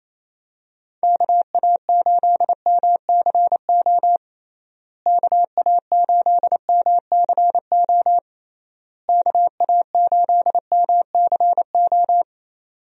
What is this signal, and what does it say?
Morse code, heard as a keyed tone, about 28 wpm, 700 Hz. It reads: KA8MCO KA8MCO KA8MCO
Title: USA Call Signs - 6 Characters - Repeated 3x - 28wpm